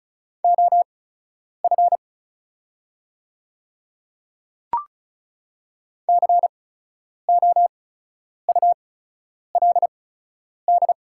Morse code, heard as a keyed tone, about 36 wpm, spaced as for 10 wpm, 700 Hz. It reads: OF COULD
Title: 8x Character Spacing - Top 100 Words - 35wpm